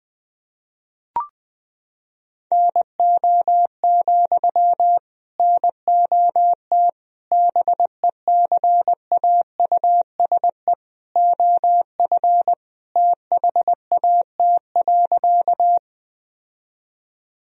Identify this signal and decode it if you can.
Morse code, heard as a keyed tone, about 20 wpm, 700 Hz. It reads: NO, NOT BECAUSE OF THAT.
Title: Sentences from Top 1000 Words - Review Part 3 - 20wpm